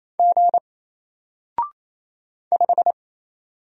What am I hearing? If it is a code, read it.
Morse code, heard as a keyed tone, about 27 wpm, 700 Hz. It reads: G 5
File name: Single-Letter-Number-28wpm